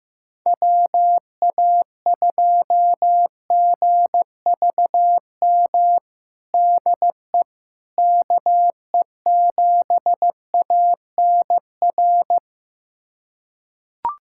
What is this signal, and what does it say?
Morse code, heard as a keyed tone, about 15 wpm, 700 Hz. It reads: WA2GVM DE KE7ANR
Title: USA Callsign DE Callsign - 6 Characters - 15wpm